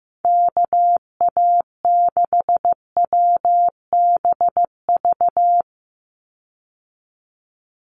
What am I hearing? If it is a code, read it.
Morse code, heard as a keyed tone, about 15 wpm, 700 Hz. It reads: KA6WBV
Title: USA Call Signs 15wpm